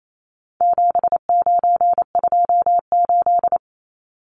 Morse code, heard as a keyed tone, about 28 wpm, 700 Hz. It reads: 7928
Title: Numbers - 4-Digit 28wpm